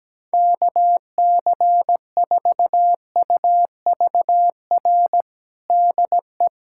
Morse code, heard as a keyed tone, about 17 wpm, 700 Hz. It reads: KC4UVR DE
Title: USA Callsign DE Callsign 17wpm